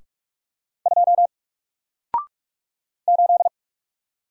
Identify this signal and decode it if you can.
Morse code, heard as a keyed tone, about 44 wpm, 700 Hz. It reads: 2 8
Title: Single Letter-Number 45wpm